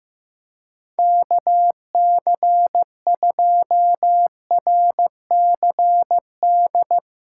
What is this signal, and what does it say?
Morse code, heard as a keyed tone, about 15 wpm, 700 Hz. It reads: KC2RCD
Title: USA Callsign DE Callsign 15wpm